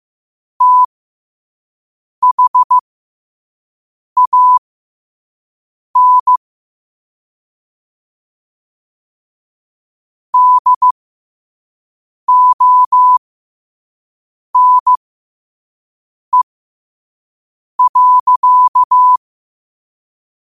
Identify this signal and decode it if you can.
Morse code, heard as a keyed tone, about 15 wpm, spaced as for 5 wpm, 1000 Hz. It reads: THAN DONE.
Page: New quotes every day in morse code at 5 Words per minute.